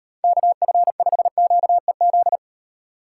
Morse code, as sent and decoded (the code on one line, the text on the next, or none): -.- ..-. ..... --.- . --..
KF5QEZ